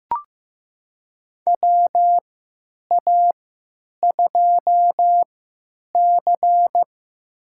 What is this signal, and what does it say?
Morse code, heard as a keyed tone, about 15 wpm, spaced as for 9 wpm, 700 Hz. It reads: WA2C
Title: USA Call Signs - 4 Characters - 3x Character Spacing - 15wpm